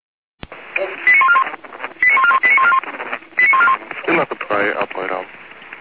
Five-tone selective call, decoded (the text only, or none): 89141 89141 89141 89141